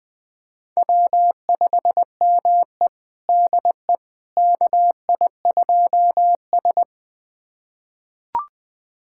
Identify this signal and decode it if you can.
Morse code, heard as a keyed tone, about 20 wpm, 700 Hz. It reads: W5ME DE KI2S
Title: USA Callsign DE Callsign - 4 Characters - 20wpm